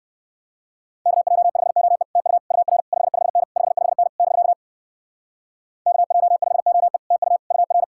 Morse code, heard as a keyed tone, about 56 wpm, 700 Hz. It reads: KJ5OE TU UR 55N 55N <BK> KJ5OE TU UR